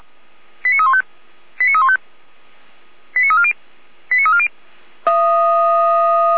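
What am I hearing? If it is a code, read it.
Five-tone selective call (ZVEI), heard as quick runs of five tones, tones of about 70 ms. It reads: 89315 89315 89340 89340
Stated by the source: Five-tone ZVEI selective call 89140, received by SDR, from feuerwehr-orlamuende fire brigade server.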